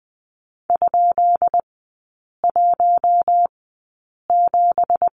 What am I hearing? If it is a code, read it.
Morse code, heard as a keyed tone, about 20 wpm, spaced as for 8 wpm, 700 Hz. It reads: ?17
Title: 172 Lesson 28e - Number 9 - Three Characters 20wpm